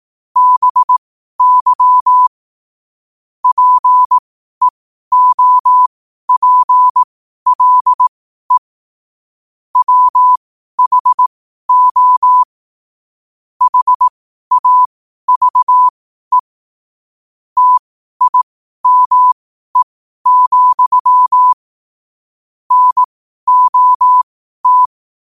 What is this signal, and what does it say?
Morse code, heard as a keyed tone, about 18 wpm, spaced as for 12 wpm, 1000 Hz. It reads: BY PEOPLE WHO HAVE TIME, NOT